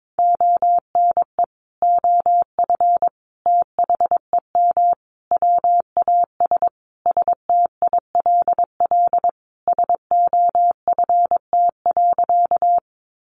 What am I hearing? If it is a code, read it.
Morse code, heard as a keyed tone, about 22 wpm, 700 Hz. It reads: ONE OF THEM WAS STILL SOFT.
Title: Sentences from Top 1500 Words 22wpm